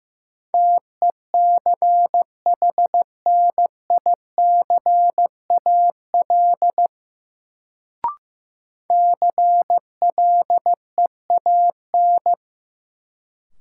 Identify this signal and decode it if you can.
Morse code, heard as a keyed tone, about 15 wpm, 700 Hz. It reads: TECHNICAL CLEAN
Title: Sentences from Top 1100 Words - New Word Drill Alt - 15wpm